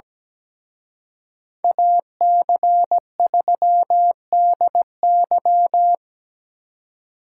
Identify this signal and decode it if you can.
Morse code, heard as a keyed tone, about 20 wpm, 700 Hz. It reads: AC3DY
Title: Speed Racing - USA 2x2 Call Signs 17wpm